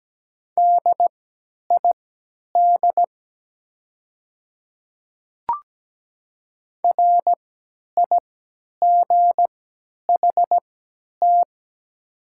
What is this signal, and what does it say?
Morse code, heard as a keyed tone, about 17 wpm, spaced as for 10 wpm, 700 Hz. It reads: DID RIGHT
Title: 3x Character Spacing - Top 200 Words - 17wpm